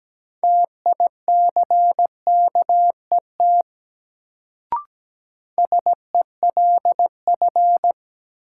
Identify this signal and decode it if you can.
Morse code, heard as a keyed tone, about 17 wpm, 700 Hz. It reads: TICKET SELF
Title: Sentences from Top 1400 Words - New Word Drill Alt - 17wpm